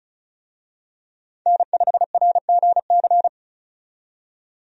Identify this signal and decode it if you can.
Morse code, heard as a keyed tone, about 36 wpm, 700 Hz. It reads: N5RGC
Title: USA Call Signs - 5 Characters - 35wpm